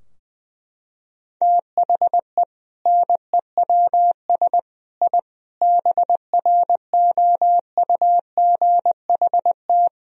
Morse code, heard as a keyed tone, about 21 wpm, 700 Hz. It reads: THE NEWS I BROUGHT
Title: Sentences from Top 1000 Words - Review Part 3 - 20wpm